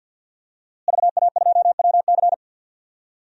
Morse code, heard as a keyed tone, about 52 wpm, 700 Hz. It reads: VA2WC
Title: DXCC-100 Call Signs - 5 Characters - Repeated 3x - 50wpm